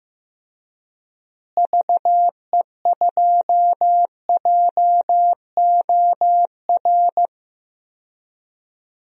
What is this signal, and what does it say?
Morse code, heard as a keyed tone, about 15 wpm, 700 Hz. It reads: VE2JOR
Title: DXCC-100 Call Signs - 6 Characters - Repeated 2x - 15wpm